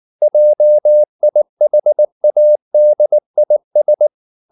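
Morse code, heard as a keyed tone, about 19 wpm, 600 Hz. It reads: JIHADIS
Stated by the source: Morse code at 19 wpm from CW med Gnister, reading Ken Follett's novel Never in English.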